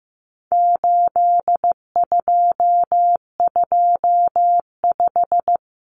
Morse code, heard as a keyed tone, about 15 wpm, 700 Hz. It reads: 8225
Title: Common Cut Numbers - 4-Digit 15wpm